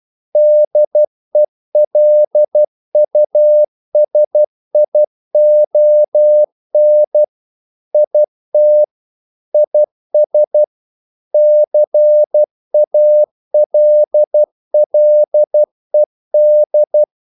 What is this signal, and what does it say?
Morse code, heard as a keyed tone, about 12 wpm, 600 Hz. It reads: DELUSION IT IS CALLED
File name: CW / Practice / EnglishQuotations0112